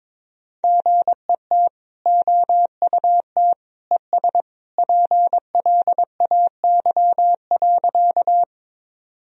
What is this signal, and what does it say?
Morse code, heard as a keyed tone, about 22 wpm, 700 Hz. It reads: GET OUT ES PLAY.
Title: Sentences from Top 200 Words and Abbreviations 22wpm